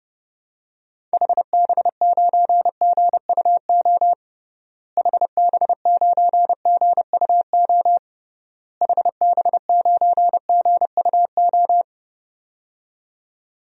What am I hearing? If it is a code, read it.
Morse code, heard as a keyed tone, about 30 wpm, 700 Hz. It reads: HB9GUO HB9GUO HB9GUO
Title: DXCC-100 Call Signs - 6 Characters - Repeated 3x - 30wpm